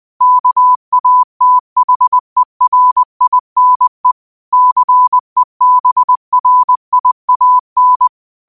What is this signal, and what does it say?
Morse code, heard as a keyed tone, about 20 wpm, 1000 Hz. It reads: KATHERINE CEBRIAN
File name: qotdfiles / 20wpm